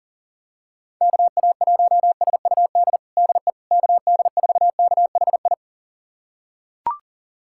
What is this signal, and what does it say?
Morse code, heard as a keyed tone, about 37 wpm, 700 Hz. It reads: KA1SUD DE KD4KHI
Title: USA Callsign DE Callsign - 6 Characters - 40wpm